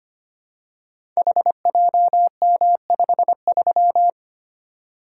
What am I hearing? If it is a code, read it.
Morse code, heard as a keyed tone, about 25 wpm, 700 Hz. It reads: HJM53
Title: Code Groups - 5 Letter:Numbers - 25wpm